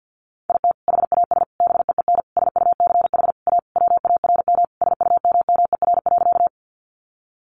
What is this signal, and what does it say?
Morse code, heard as a keyed tone, about 48 wpm, 700 Hz. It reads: IT HAS BEEN SUCH A WARM SUMMER.
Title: Sentences from Top 1100 Words 50wpm